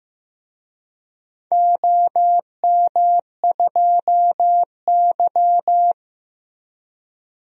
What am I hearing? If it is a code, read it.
Morse code, heard as a keyed tone, about 16 wpm, 700 Hz. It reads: OM2Y
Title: DXCC-100 Call Signs - 4 Characters - Repeated 3x - 15wpm